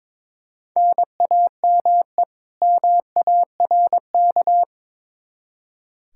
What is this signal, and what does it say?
Morse code, heard as a keyed tone, about 21 wpm, 700 Hz. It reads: NAME MARK
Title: Practice QSO - Exchange 1 - Beginning - 22wpm